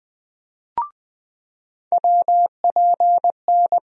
Morse code, heard as a keyed tone, about 21 wpm, 700 Hz. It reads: WPN
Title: USA Call Sign Postfixes - 3 Letters 20wpm